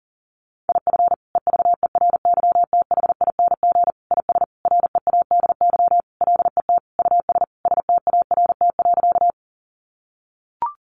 Morse code, heard as a keyed tone, about 40 wpm, 700 Hz. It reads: IF EVERYTHING IS READY LET US START.
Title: Sentences from Top 700 Words 40wpm